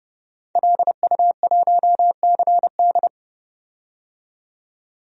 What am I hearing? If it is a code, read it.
Morse code, heard as a keyed tone, about 30 wpm, 700 Hz. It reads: LU1CD
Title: DXCC-100 Call Signs - 5 Characters - 30wpm